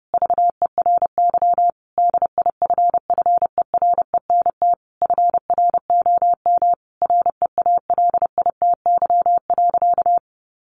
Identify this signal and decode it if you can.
Morse code, heard as a keyed tone, about 30 wpm, 700 Hz. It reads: VERY DIFFERENT FROM REALITY.
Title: Sentences from Top 1500 Words 30wpm